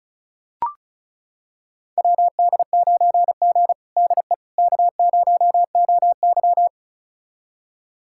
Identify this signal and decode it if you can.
Morse code, heard as a keyed tone, about 36 wpm, 700 Hz. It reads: WD9G DE K0OY
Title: USA Callsign DE Callsign - 4 Characters - 35wpm